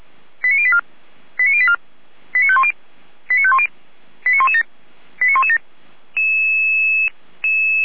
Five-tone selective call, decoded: 89084 89084 89510 89510 89117 89117

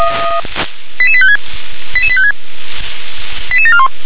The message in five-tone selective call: 88856 88856 88741

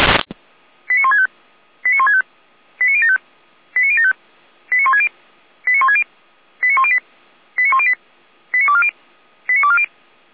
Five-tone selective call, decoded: 89165 89165 89075 89075 89160 89160 89108 89108 89240 89240

